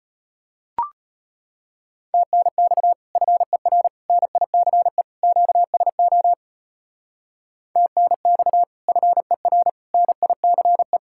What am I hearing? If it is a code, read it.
Morse code, heard as a keyed tone, about 30 wpm, 700 Hz. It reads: TNX FER NICE QSO TNX FER NICE
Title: Speed-Racing Common QSO Phrases - 25wpm